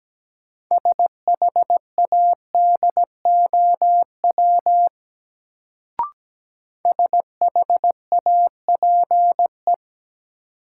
Morse code, heard as a keyed tone, about 17 wpm, 700 Hz. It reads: SHADOW SHAPE
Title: Sets of 5 Words Encore - New Word Drill - 17wpm